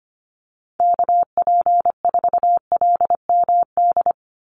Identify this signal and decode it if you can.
Morse code, heard as a keyed tone, about 25 wpm, 700 Hz. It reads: KP4LMD
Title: USA Call Signs 25wpm